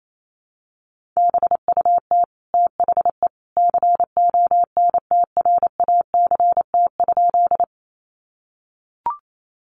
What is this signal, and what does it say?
Morse code, heard as a keyed tone, about 28 wpm, 700 Hz. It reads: BUT THE CONTRACT?
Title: Sentences from Top 1400 Words 28wpm